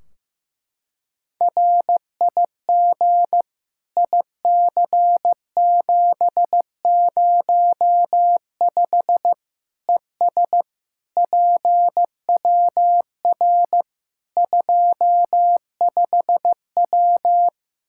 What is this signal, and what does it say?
Morse code, heard as a keyed tone, about 15 wpm, 700 Hz. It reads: RIG IC705 ES PWR 25W
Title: Practice QSO - Exchange 2 - Info - 15wpm